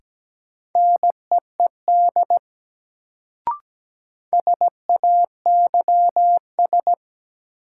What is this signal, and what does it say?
Morse code, heard as a keyed tone, about 17 wpm, 700 Hz. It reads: NEED SAYS
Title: Sets of 2 Words - New Word Drill - 17wpm